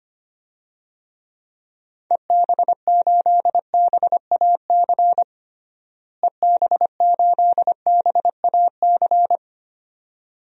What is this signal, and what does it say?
Morse code, heard as a keyed tone, about 25 wpm, 700 Hz. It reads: EB8BAC EB8BAC
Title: DXCC-100 Call Signs - 6 Characters - Repeated 2x - 25wpm